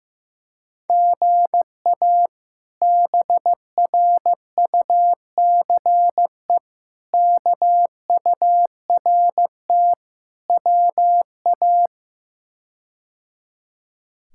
Morse code, heard as a keyed tone, about 15 wpm, 700 Hz. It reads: GA BRUCE KURT WA
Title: K1USN SST QSO Practice - 15wpm